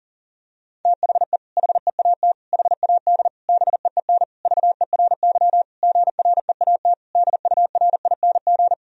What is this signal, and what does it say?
Morse code, heard as a keyed tone, about 40 wpm, 700 Hz. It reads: THE HEAT HAD BEEN VERY GREAT DURING
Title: Sentences from Top 1000 Words - Review Part 3 - 40wpm